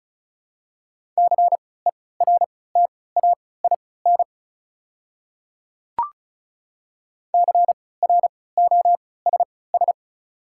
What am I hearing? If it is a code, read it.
Morse code, heard as a keyed tone, about 36 wpm, spaced as for 20 wpm, 700 Hz. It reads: CERTAIN CROSS
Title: 3x Character Spacing - Top 500 Words - 35wpm